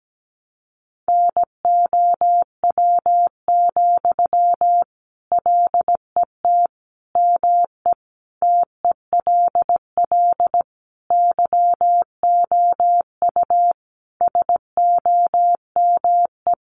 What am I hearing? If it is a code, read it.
Morse code, heard as a keyed tone, about 17 wpm, 700 Hz. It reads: NOW, LET ME TELL YOU SOME
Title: Sentences from Top 800 Words 17wpm